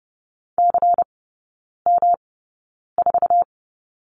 Morse code, heard as a keyed tone, about 30 wpm, spaced as for 9 wpm, 700 Hz. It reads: CM4
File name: Lesson 18e - Letter m - Three Characters 30wpm